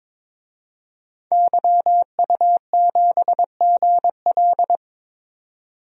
Morse code, heard as a keyed tone, about 21 wpm, 700 Hz. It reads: YU7GL
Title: DXCC-100 Call Signs - 5 Characters - 22wpm